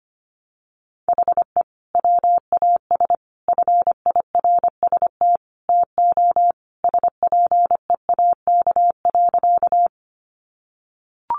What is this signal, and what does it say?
Morse code, heard as a keyed tone, about 25 wpm, 700 Hz. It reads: HE WAS FIRST TO SPEAK.
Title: Sentences from Top 1100 Words 25wpm